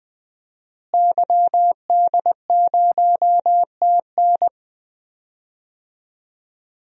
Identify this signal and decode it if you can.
Morse code, heard as a keyed tone, about 20 wpm, 700 Hz. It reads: YD0TN
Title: DXCC-100 Call Signs - 5 Characters - Repeated 3x - 20wpm